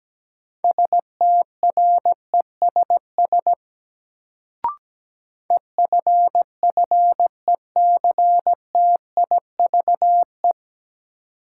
Morse code, heard as a keyed tone, about 17 wpm, 700 Hz. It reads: STRESS EFFECTIVE
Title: Sentences from Top 1000 Words - Review Part 6 - New Word Drill - 17wpm